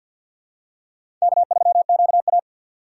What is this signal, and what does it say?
Morse code, heard as a keyed tone, about 51 wpm, 700 Hz. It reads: K3QA
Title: USA Call Signs - 4 Characters - 2x Character Spacing - 50wpm